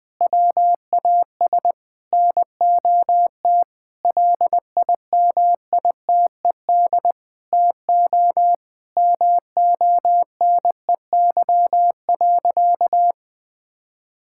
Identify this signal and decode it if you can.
Morse code, heard as a keyed tone, about 20 wpm, 700 Hz. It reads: WAS NOT LIMITED TO MONEY.
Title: Sentences from Top 1000 Words - Review Part 6 - 20wpm